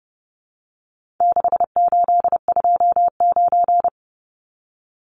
Morse code, heard as a keyed tone, about 30 wpm, 700 Hz. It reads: 6829
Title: Numbers - 4-Digit 30wpm